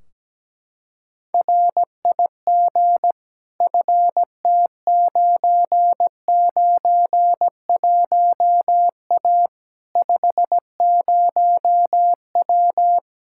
Morse code, heard as a keyed tone, about 17 wpm, 700 Hz. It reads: RIG FT991A 50W
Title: Practice QSO - Exchange 2 - Info - 17wpm